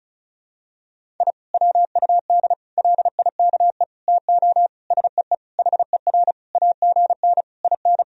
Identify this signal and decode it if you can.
Morse code, heard as a keyed tone, about 35 wpm, 700 Hz. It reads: I WUD LIKE TO SEE HER AGN IN
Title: Sentences from Top 200 Words and Abbreviations 35wpm